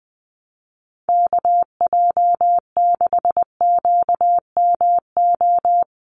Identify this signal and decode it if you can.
Morse code, heard as a keyed tone, about 20 wpm, 700 Hz. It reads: KJ6QMO
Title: USA Call Signs 20wpm